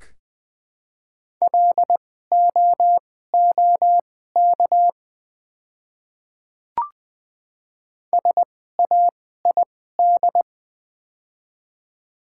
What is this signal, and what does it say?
Morse code, heard as a keyed tone, about 20 wpm, spaced as for 14 wpm, 700 Hz. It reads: LOOK SAID
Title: 2x Character Spacing - Top 100 Words - 20wpm